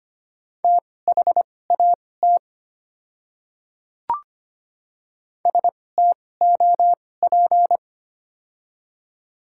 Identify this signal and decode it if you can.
Morse code, heard as a keyed tone, about 25 wpm, spaced as for 18 wpm, 700 Hz. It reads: THAT STOP
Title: 2x Character Spacing - Top 300 Words - 25wpm